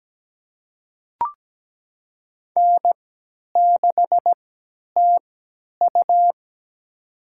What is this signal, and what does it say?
Morse code, heard as a keyed tone, about 17 wpm, spaced as for 10 wpm, 700 Hz. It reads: N6TU
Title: USA Call Signs - 4 Characters - 3x Character Spacing - 17wpm